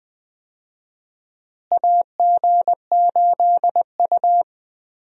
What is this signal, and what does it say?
Morse code, heard as a keyed tone, about 20 wpm, 700 Hz. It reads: AG8U
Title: USA Call Signs - 4 Characters - Repeated 3x - 20wpm